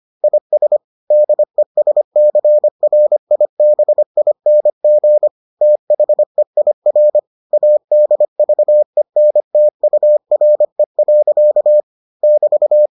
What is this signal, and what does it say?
Morse code, heard as a keyed tone, about 25 wpm, 600 Hz. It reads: IS DESCRIBING THEIR ADVENTURE. <BT>